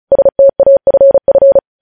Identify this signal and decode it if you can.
Morse code, heard as a keyed tone, about 36 wpm, 550 Hz. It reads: STAFF
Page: News Headlines in Morse Code at 35 WPM